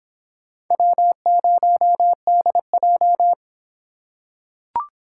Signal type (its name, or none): Morse code